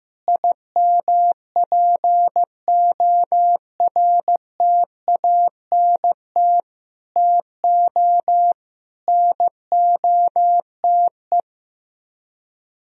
Morse code, heard as a keyed tone, about 15 wpm, 700 Hz. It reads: IMPORTANT TO NOTE
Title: Speed-Racing - Sets of 5 Words 15wpm